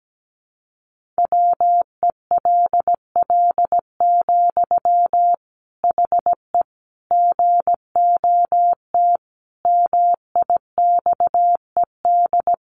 Morse code, heard as a keyed tone, about 17 wpm, 700 Hz. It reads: WELL, HE GOT MIXED
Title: Sentences from Top 1500 Words 17wpm